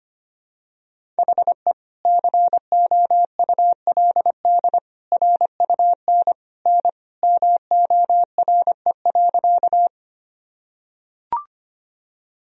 Morse code, heard as a keyed tone, about 25 wpm, 700 Hz. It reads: HE COULD RUN N MORE.
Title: Sentences from Top 300 Words and Abbreviations 25wpm